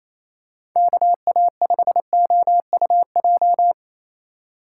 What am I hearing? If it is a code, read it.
Morse code, heard as a keyed tone, about 27 wpm, 700 Hz. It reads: KA5OUJ
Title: USA Call Signs - 6 Characters - 3x Character Spacing - 28wpm